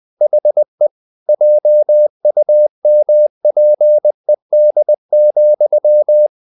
Morse code, heard as a keyed tone, about 20 wpm, 600 Hz. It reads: HE JUMPED,
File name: the old man and the sea / the old man and the sea part 1 20wpm